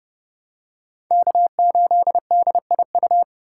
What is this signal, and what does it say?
Morse code, heard as a keyed tone, about 27 wpm, 700 Hz. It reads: K8DIU